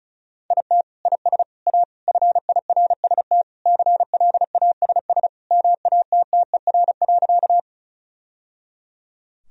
Morse code, heard as a keyed tone, about 34 wpm, 700 Hz. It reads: IT IS A FIRST CLASS MATTER.
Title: Sentences from Top 1000 Words - Review Part 2 - 35wpm